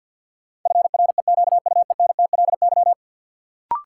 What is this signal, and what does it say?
Morse code, heard as a keyed tone, about 53 wpm, 700 Hz. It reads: FREQUENTLY
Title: Sentences from Top 1300 Words - New Word Drill - 50wpm